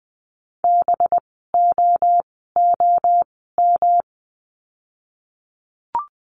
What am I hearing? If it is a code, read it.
Morse code, heard as a keyed tone, about 20 wpm, spaced as for 15 wpm, 700 Hz. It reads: BOOM
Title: 185 Lesson 30f - Letter b - Words 20wpm